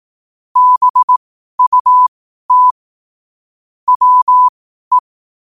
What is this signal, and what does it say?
Morse code, heard as a keyed tone, about 18 wpm, spaced as for 12 wpm, 1000 Hz. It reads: BUT WE
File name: qotdfiles / 12wpm